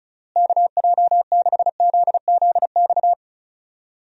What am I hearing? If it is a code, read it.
Morse code, heard as a keyed tone, about 36 wpm, 700 Hz. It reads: KJ6ZZX
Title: USA Call Signs - 6 Characters - Repeated 2x - 35wpm